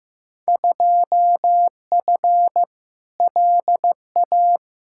Morse code, heard as a keyed tone, about 15 wpm, 700 Hz. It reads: 2F LA
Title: Field Day Exchanges - 15wpm